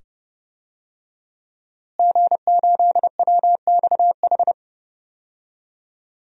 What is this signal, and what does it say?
Morse code, heard as a keyed tone, about 29 wpm, 700 Hz. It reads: G8WXH
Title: DXCC-100 Call Signs - 5 Characters - Repeated 2x - 30wpm